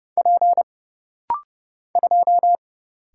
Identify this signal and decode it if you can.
Morse code, heard as a keyed tone, about 31 wpm, 700 Hz. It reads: P 2
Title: Single Letter-Number - Rapid-Fire 30wpm